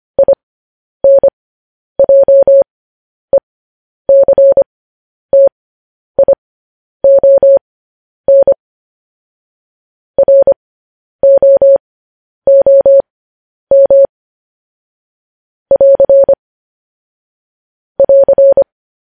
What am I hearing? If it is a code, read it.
Morse code, heard as a keyed tone, about 25 wpm, spaced as for 10 wpm, 550 Hz. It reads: INJECTION ROOM <AR> <AR>
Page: news-headlines-in-morse-code-at-10-wpm